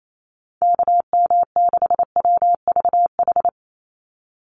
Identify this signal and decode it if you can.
Morse code, heard as a keyed tone, about 28 wpm, 700 Hz. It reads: KM6WVH
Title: USA Call Signs 28wpm